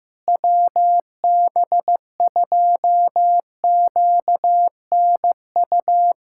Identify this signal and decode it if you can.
Morse code, heard as a keyed tone, about 15 wpm, 700 Hz. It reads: WB2QNU
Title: K1USN SST QSO Practice - 15wpm